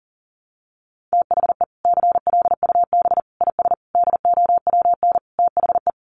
Morse code, heard as a keyed tone, about 40 wpm, 700 Hz. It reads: THE CLUB IS DOWN THE
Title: Sentences from Top 1200 Words 40wpm